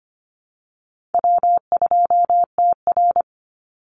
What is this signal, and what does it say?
Morse code, heard as a keyed tone, about 25 wpm, 700 Hz. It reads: W2TR
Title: Call Signs - 25wpm